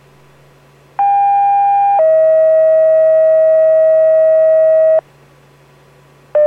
Two-tone page, heard as a one-second tone, then a three-second tone, about 800 then 640 Hz. It is heard from 1.0 to 5.0 s.